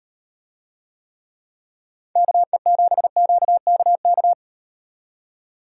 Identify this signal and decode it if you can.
Morse code, heard as a keyed tone, about 38 wpm, 700 Hz. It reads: KE7QKK